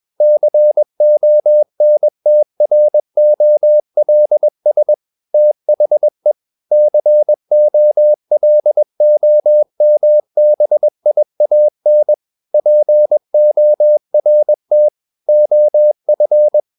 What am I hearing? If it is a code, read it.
Morse code, heard as a keyed tone, about 21 wpm, 600 Hz. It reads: CONTROLS THE COLOMBIAN PORT OF
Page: CW med Gnister